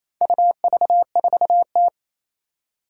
Morse code, heard as a keyed tone, about 28 wpm, 700 Hz. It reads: UV4T
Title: Four Challenging Letter-Number Combinations 28wpm